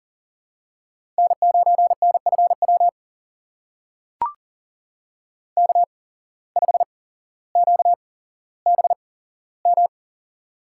Morse code, heard as a keyed tone, about 40 wpm, spaced as for 17 wpm, 700 Hz. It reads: N9NFW K5QBM
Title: USA Call Signs - 5 Characters - 8x Character Spacing - 40wpm